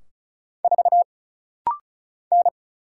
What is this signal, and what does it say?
Morse code, heard as a keyed tone, about 36 wpm, 700 Hz. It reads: 4 N
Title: Single Letter-Number Alt - Rapid-Fire 35wpm